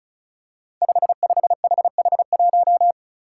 Morse code, heard as a keyed tone, about 34 wpm, 700 Hz. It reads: 55HH1
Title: Five Challenging Letter-Number Combinations 35wpm